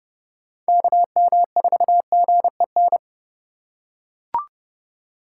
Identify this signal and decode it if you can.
Morse code, heard as a keyed tone, about 30 wpm, 700 Hz. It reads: KM4GEN
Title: USA Call Signs - 3x Character Spacing - 30wpm